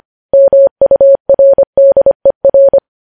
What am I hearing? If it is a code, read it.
Morse code, heard as a keyed tone, about 25 wpm, 550 Hz. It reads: MURDER
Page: news headlines in morse code at 25 wpm